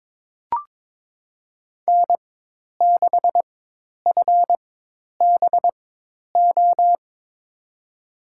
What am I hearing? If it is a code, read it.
Morse code, heard as a keyed tone, about 22 wpm, spaced as for 10 wpm, 700 Hz. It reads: N6FBO
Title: USA Call Signs - 5 Characters - 4x Character Spacing - 22wpm